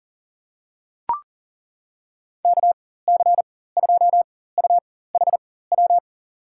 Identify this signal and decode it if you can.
Morse code, heard as a keyed tone, about 40 wpm, spaced as for 19 wpm, 700 Hz. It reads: KC2UHW
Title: USA Call Signs - 6 Characters - 4x Character Spacing - 40wpm